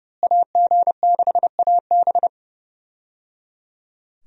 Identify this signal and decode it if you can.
Morse code, heard as a keyed tone, about 30 wpm, 700 Hz. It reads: AG6AB
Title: USA Call Signs - 5 Characters - Repeated 3x - 30wpm